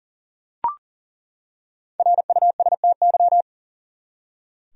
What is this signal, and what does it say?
Morse code, heard as a keyed tone, about 40 wpm, 700 Hz. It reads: RUSTY
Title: USA Common Names - Max 5 Letters - 40wpm